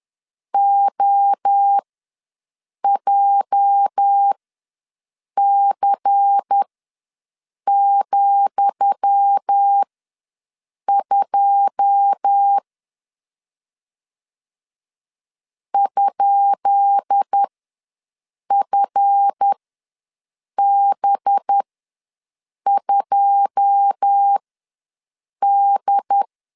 Morse code, heard as a keyed tone, about 10 wpm, spaced as for 6 wpm, 800 Hz. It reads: OJC,2 ?FB2D
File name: LEZIONI / 10 SIGNS / 10D COMA ws 10wpm 4314